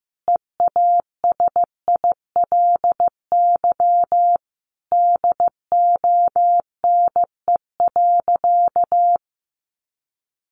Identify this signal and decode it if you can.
Morse code, heard as a keyed tone, about 15 wpm, 700 Hz. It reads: EASILY DONE.
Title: Sentences from Top 900 Words 15wpm